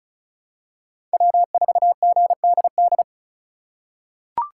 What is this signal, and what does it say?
Morse code, heard as a keyed tone, about 36 wpm, 700 Hz. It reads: W4GDD
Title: USA Call Signs - 5 Characters - 4x Character Spacing - 35wpm